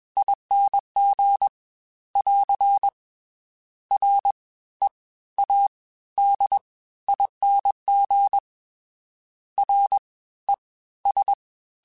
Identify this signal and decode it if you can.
Morse code, heard as a keyed tone, about 21 wpm, spaced as for 15 wpm, 800 Hz. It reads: ING <AR> READING RES